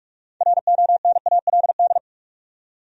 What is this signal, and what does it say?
Morse code, heard as a keyed tone, about 46 wpm, 700 Hz. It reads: RONALD
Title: USA Common Names - 45wpm